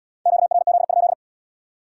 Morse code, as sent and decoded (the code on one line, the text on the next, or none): --.- .-. --.. ..--..
QRZ?